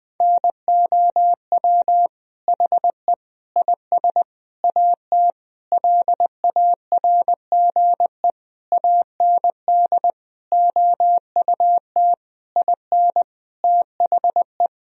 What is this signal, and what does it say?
Morse code, heard as a keyed tone, about 21 wpm, 700 Hz. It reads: NOW HE IS AT LARGE AND OUT IN THE
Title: Sentences from Top 1000 Words - Review Part 1 - 20wpm